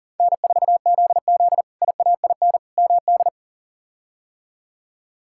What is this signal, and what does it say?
Morse code, heard as a keyed tone, about 40 wpm, 700 Hz. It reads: N4ZZ IAIN MD
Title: K1USN SST QSO Practice - 40wpm